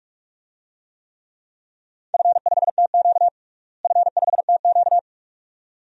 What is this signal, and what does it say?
Morse code, heard as a keyed tone, about 41 wpm, 700 Hz. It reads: F5TQ F5TQ
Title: DXCC-100 Call Signs - 4 Characters - Repeated 2x - 45wpm